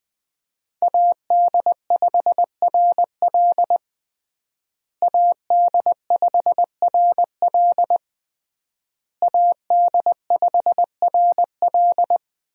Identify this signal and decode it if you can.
Morse code, heard as a keyed tone, about 21 wpm, 700 Hz. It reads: AD5RL AD5RL AD5RL
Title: USA Call Signs - 5 Characters - Repeated 3x - 20wpm